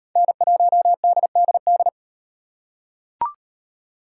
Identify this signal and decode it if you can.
Morse code, heard as a keyed tone, about 38 wpm, 700 Hz. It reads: N1DDD